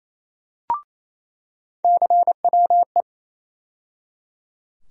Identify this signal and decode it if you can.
Morse code, heard as a keyed tone, about 28 wpm, 700 Hz. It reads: CWE
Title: USA Call Sign Postfixes - 3 Letters 28wpm